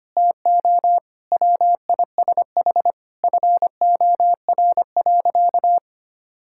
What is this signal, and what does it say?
Morse code, heard as a keyed tone, about 25 wpm, 700 Hz. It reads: TO WISH FOR.
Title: Sentences from Top 1600 Words 25wpm